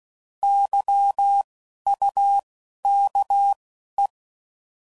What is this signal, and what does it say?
Morse code, heard as a keyed tone, about 16 wpm, 800 Hz. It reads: Y U K E
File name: Prefixes / By First Letter / Ys